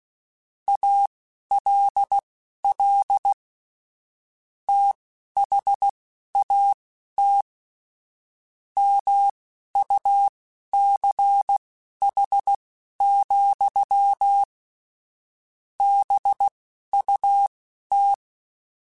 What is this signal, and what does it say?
Morse code, heard as a keyed tone, about 16 wpm, spaced as for 11 wpm, 800 Hz. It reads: ALL THAT MUCH, BUT
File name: QSOs / QSOs-mp3s / WD0BGP 2@10